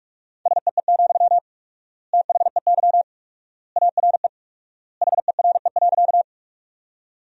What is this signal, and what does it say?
Morse code, heard as a keyed tone, about 42 wpm, 700 Hz. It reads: SEE, THEY ARE HERE.
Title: 4x Word-Spacing - Sentences from Top 200 Words - 45wpm